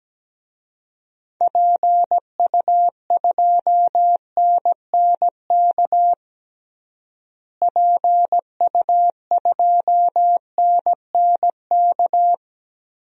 Morse code, heard as a keyed tone, about 17 wpm, 700 Hz. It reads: PU2NNK PU2NNK
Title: DXCC-100 Call Signs - 6 Characters - Repeated 2x - 17wpm